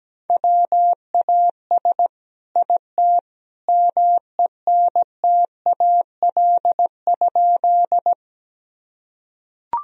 Morse code, heard as a keyed tone, about 17 wpm, 700 Hz. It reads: WAS IT MENTAL?
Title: Sentences from Top 1000 Words - Review Part 4 - 17wpm